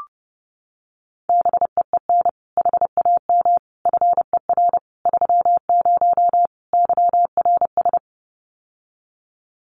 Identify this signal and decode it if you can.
Morse code, heard as a keyed tone, about 30 wpm, 700 Hz. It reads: BEEN HAM FER 30 YRS
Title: Common QSO Phrases Encore 30wpm